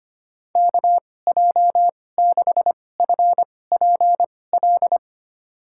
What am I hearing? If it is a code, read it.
Morse code, heard as a keyed tone, about 25 wpm, spaced as for 18 wpm, 700 Hz. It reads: KJ6FPL
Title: USA Call Signs - 6 Characters - 2x Character Spacing - 25wpm